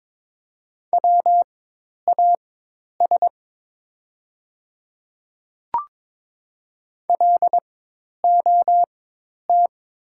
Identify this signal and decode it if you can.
Morse code, heard as a keyed tone, about 21 wpm, spaced as for 10 wpm, 700 Hz. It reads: WAS LOT
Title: 4x Character Spacing - Top 500 Words - 22wpm